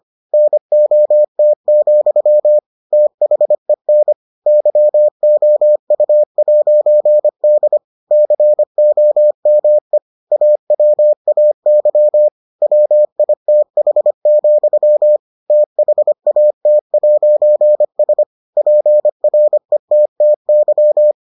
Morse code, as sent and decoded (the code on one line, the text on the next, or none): -. --- - --..-- / - .... -. / -.-- --- ..- .----. -.. / -.-. --- -- . / .- .-- .- -.-- / .-- .. - .... --..-- / - .... .- - .----. ... / .--. .-. - - -.--
NOT, THEN YOU'D COME AWAY WITH, THAT'S PRETTY